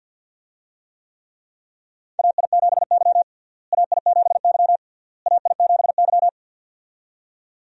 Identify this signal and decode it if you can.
Morse code, heard as a keyed tone, about 51 wpm, 700 Hz. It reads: AI7Y AI7Y AI7Y
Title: USA Call Signs - 4 Characters - Repeated 3x - 50wpm